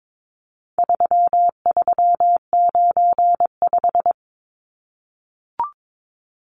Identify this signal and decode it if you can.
Morse code, heard as a keyed tone, about 22 wpm, 700 Hz. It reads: 3395
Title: Numbers - 4-Digit 22wpm